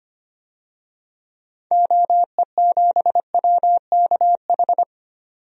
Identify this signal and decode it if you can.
Morse code, heard as a keyed tone, about 25 wpm, 700 Hz. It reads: OE7WKH
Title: DXCC-100 Call Signs - 6 Characters - 25wpm